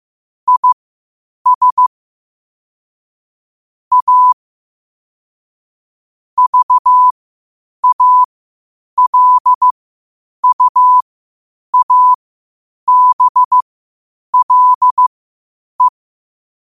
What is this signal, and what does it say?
Morse code, heard as a keyed tone, about 15 wpm, spaced as for 8 wpm, 1000 Hz. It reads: IS A VALUABLE
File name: qotdfiles / 8wpm